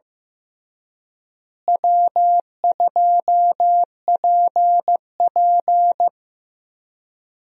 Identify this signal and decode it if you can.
Morse code, heard as a keyed tone, about 15 wpm, 700 Hz. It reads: W2PP
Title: USA Call Signs - 4 Characters - Repeated 3x - 15wpm